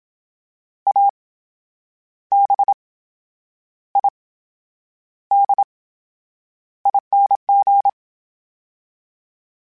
Morse code, heard as a keyed tone, about 27 wpm, spaced as for 9 wpm, 800 Hz. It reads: ABIDING
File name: Special INGs / Special INGs-mp3s / VerySlowToFaster Ing @25